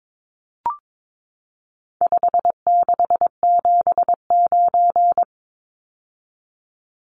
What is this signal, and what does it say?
Morse code, heard as a keyed tone, about 22 wpm, 700 Hz. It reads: 5679
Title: Numbers - 4-Digit 22wpm